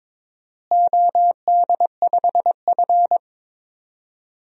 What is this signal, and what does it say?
Morse code, heard as a keyed tone, about 22 wpm, 700 Hz. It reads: OD5F
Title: DXCC-100 Call Signs - 4 Characters - Repeated 3x - 22wpm